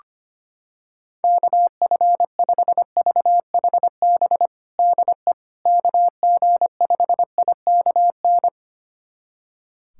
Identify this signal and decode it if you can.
Morse code, heard as a keyed tone, about 25 wpm, 700 Hz. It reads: KF5VHB DE KG5IKN
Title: USA Callsign DE Callsign - 6 Characters - 25wpm